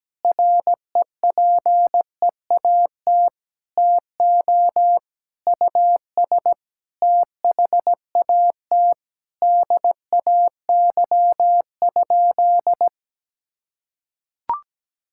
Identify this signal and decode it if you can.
Morse code, heard as a keyed tone, about 17 wpm, 700 Hz. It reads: REPEAT TO US THAT DAY?
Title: Sentences from Top 1600 Words 17wpm